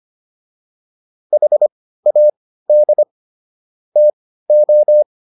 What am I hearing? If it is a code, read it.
Morse code, heard as a keyed tone, about 25 wpm, spaced as for 15 wpm, 600 Hz. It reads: HAD TO